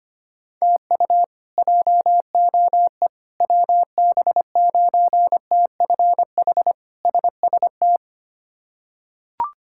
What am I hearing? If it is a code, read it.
Morse code, heard as a keyed tone, about 25 wpm, 700 Hz. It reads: TU JOE WB9TFH SST
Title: K1USN SST QSO Practice - 25wpm